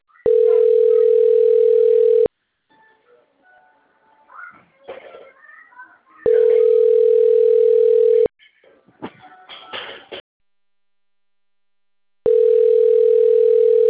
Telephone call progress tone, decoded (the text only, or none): ringback tone